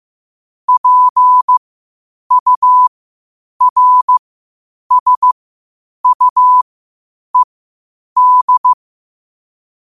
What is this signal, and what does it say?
Morse code, heard as a keyed tone, about 15 wpm, spaced as for 8 wpm, 1000 Hz. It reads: PURSUED
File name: qotdfiles / 8wpm